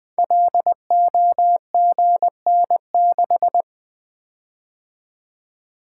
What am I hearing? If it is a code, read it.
Morse code, heard as a keyed tone, about 20 wpm, 700 Hz. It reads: LOGN6
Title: Code Groups - 5 Letter:Numbers - 20wpm